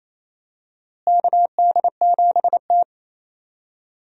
Morse code, heard as a keyed tone, about 28 wpm, 700 Hz. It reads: KD7T
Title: USA Call Signs - 4 Characters - Repeated 2x - 28wpm